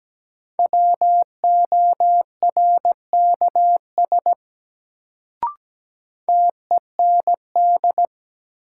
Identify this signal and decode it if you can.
Morse code, heard as a keyed tone, about 17 wpm, 700 Hz. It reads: WORKS TEND
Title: Sentences from Top 1000 Words - Review Part 5 - New Word Drill - 17wpm